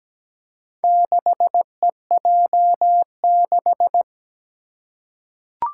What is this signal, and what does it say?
Morse code, heard as a keyed tone, about 17 wpm, 700 Hz. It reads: 6EJ6
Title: Four Challenging Letter-Number Combinations 17wpm